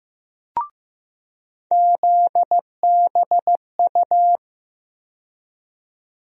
Morse code, heard as a keyed tone, about 15 wpm, 700 Hz. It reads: ZBU
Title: USA Call Sign Postfixes - 3 Letters 15wpm